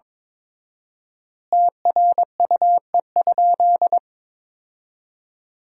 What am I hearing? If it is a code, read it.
Morse code, heard as a keyed tone, about 22 wpm, 700 Hz. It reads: TRUE?
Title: 4x Word-Spacing - Sentences from Top 400 Words - 22wpm